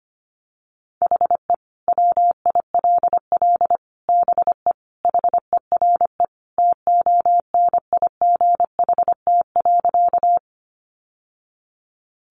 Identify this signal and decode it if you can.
Morse code, heard as a keyed tone, about 25 wpm, 700 Hz. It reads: HE WILL BE HERE TONIGHT.
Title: Sentences from Top 600 Words 25wpm